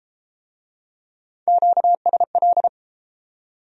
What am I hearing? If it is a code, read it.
Morse code, heard as a keyed tone, about 28 wpm, 700 Hz. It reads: QSL